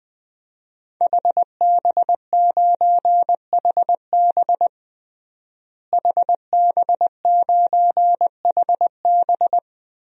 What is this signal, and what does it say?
Morse code, heard as a keyed tone, about 20 wpm, 700 Hz. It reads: HB9HB HB9HB
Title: DXCC-100 Call Signs - 5 Characters - Repeated 2x - 20wpm